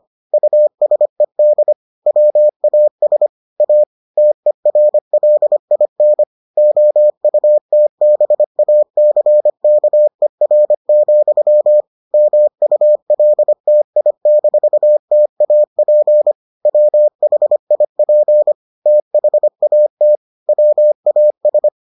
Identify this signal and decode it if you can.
Morse code, heard as a keyed tone, about 25 wpm, 600 Hz. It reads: USED WAS A TERLIN OUTBACKER, MULTI-TAP WHIP THAT WAS